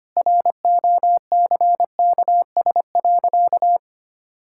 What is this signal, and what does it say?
Morse code, heard as a keyed tone, about 25 wpm, 700 Hz. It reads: ROCKS.
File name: Easy Sentences with Hints - 25wpm